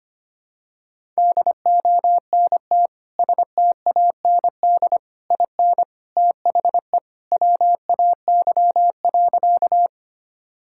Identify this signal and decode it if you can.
Morse code, heard as a keyed tone, about 25 wpm, 700 Hz. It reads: DONT STAND IN THE WAY.
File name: practice / 25wpm / Sentences with 3-Word Phrases - 25wpm